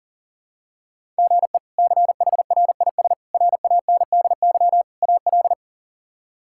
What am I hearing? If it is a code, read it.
Morse code, heard as a keyed tone, about 40 wpm, 700 Hz. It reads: GE CHRIS RANDY AL
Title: K1USN SST QSO Practice - 40wpm